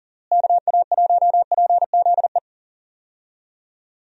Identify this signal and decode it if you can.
Morse code, heard as a keyed tone, about 40 wpm, 700 Hz. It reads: KA1PZE
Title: USA Call Signs - 6 Characters - Repeated 3x - 40wpm